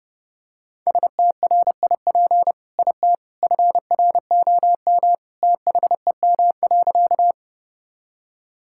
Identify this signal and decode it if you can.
Morse code, heard as a keyed tone, about 30 wpm, 700 Hz. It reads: STRIP IT FROM THEM.
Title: Sentences from Top 1600 Words 30wpm